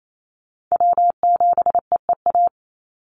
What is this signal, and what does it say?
Morse code, heard as a keyed tone, about 28 wpm, 700 Hz. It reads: W7EEA
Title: USA Call Signs 28wpm